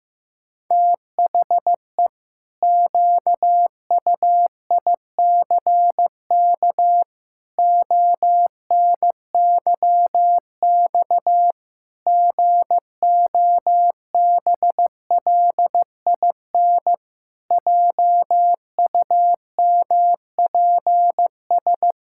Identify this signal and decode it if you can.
Morse code, heard as a keyed tone, about 15 wpm, 700 Hz. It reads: THE QUICK ONYX GOBLIN JUMPS
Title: Pangram Challenge v2 - 15wpm